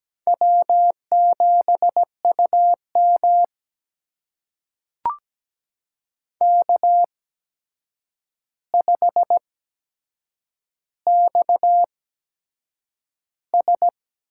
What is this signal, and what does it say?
Morse code, heard as a keyed tone, about 17 wpm, spaced as for 7 wpm, 700 Hz. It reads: W7UM K5XS
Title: USA Call Signs - 4 Characters - 8x Character Spacing - 17wpm